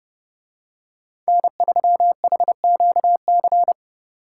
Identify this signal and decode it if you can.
Morse code, heard as a keyed tone, about 28 wpm, 700 Hz. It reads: N3HQC